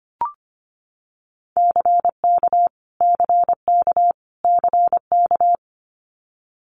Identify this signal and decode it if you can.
Morse code, heard as a keyed tone, about 25 wpm, 700 Hz. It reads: CK CK CK
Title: Multi-letter Phonemes Repeated 3x 25wpm